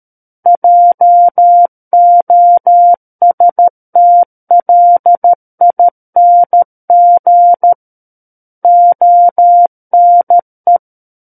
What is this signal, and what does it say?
Morse code, heard as a keyed tone, about 13 wpm, 700 Hz. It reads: JOSTLING ONE
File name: Book One / 04-Chapter 4 - 13 WPM